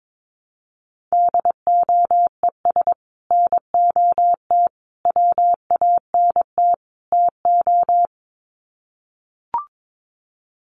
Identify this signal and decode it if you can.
Morse code, heard as a keyed tone, about 22 wpm, 700 Hz. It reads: DOES NOT WANT TO
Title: Sets of 4 Words 22wpm